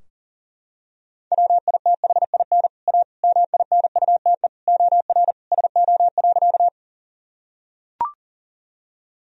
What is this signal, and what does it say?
Morse code, heard as a keyed tone, about 37 wpm, 700 Hz. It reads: WITHIN A MINUTE OR SO.
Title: Sentences from Top 1000 Words - Review Part 4 - 40wpm